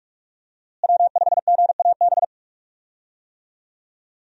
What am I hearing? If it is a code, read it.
Morse code, heard as a keyed tone, about 46 wpm, 700 Hz. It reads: W5GAB
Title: USA Call Signs - 5 Characters - Repeated 2x - 45wpm